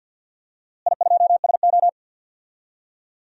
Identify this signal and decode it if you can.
Morse code, heard as a keyed tone, about 50 wpm, 700 Hz. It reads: I2SO
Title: DXCC-100 Call Signs - 4 Characters - Repeated 3x - 50wpm